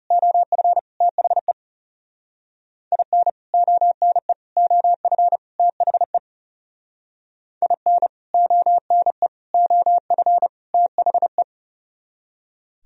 Morse code, heard as a keyed tone, about 37 wpm, 700 Hz. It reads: OF THE IN ONE OF THE IN ONE OF THE
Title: Speed-Racing - Sets of 4 Words 30wpm